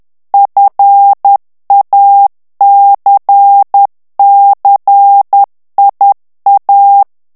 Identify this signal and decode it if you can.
Morse code, heard as a keyed tone, about 10 wpm, 800 Hz. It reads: FACCIA
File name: morse mp3 / ITALIEN texte clair 1 vitesse 10 mots minute